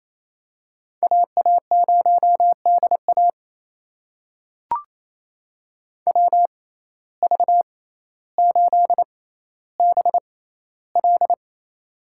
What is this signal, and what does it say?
Morse code, heard as a keyed tone, about 29 wpm, spaced as for 15 wpm, 700 Hz. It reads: AA0DA WV8BL
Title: USA Call Signs - 5 Characters - 6x Character Spacing - 28wpm